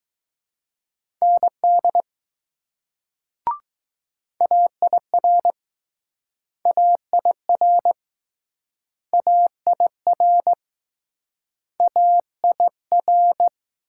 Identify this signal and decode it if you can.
Morse code, heard as a keyed tone, about 20 wpm, 700 Hz. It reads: ND AIR AIR AIR AIR